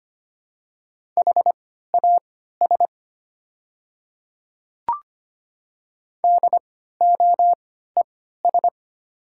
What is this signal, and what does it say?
Morse code, heard as a keyed tone, about 25 wpm, spaced as for 14 wpm, 700 Hz. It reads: HAS DOES
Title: 3x Character Spacing - Top 200 Words - 25wpm